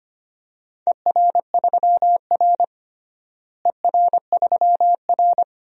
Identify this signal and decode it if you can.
Morse code, heard as a keyed tone, about 25 wpm, 700 Hz. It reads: ER3R ER3R
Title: DXCC-100 Call Signs - 4 Characters - Repeated 3x - 25wpm